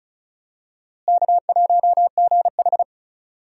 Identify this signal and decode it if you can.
Morse code, heard as a keyed tone, about 36 wpm, 700 Hz. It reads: K1GH
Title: USA Call Signs - 4 Characters - Repeated 3x - 35wpm